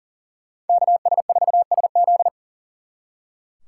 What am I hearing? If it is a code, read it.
Morse code, heard as a keyed tone, about 38 wpm, 700 Hz. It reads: KS4SZ
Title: Speed Racing - USA 2x2 Call Signs 40wpm